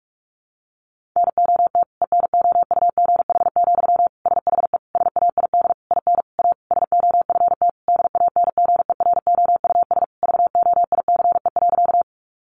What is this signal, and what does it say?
Morse code, heard as a keyed tone, about 44 wpm, 700 Hz. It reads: NOT ENOUGH, SHE SAID IN A SOFT DANGEROUS VOICE.
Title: Sentences from Top 1500 Words 45wpm